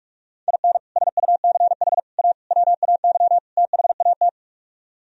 Morse code, heard as a keyed tone, about 45 wpm, 700 Hz. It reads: IN SUCH A WAY THAT